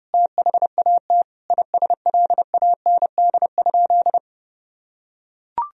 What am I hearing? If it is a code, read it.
Morse code, heard as a keyed tone, about 31 wpm, 700 Hz. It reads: THAT ISLAND?
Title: Sentences from Top 500 Words and Abbreviations 30wpm